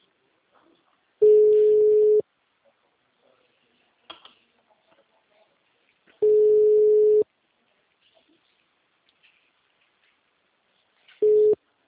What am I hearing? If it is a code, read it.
Ringback tone (Europe, 425 Hz).
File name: out- -2017- - - .52021